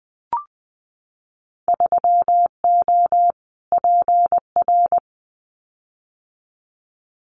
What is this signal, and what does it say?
Morse code, heard as a keyed tone, about 20 wpm, 700 Hz. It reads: 3O PR
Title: Winter Field Day Exchanges Encore 20wpm